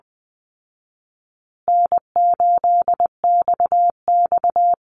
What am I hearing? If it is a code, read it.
Morse code, heard as a keyed tone, about 20 wpm, 700 Hz. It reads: N8XX
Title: Call Signs Repeated 2x - 20wpm